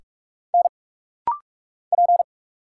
Morse code, heard as a keyed tone, about 44 wpm, 700 Hz. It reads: N P
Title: Single Letter-Number - Rapid-Fire 45wpm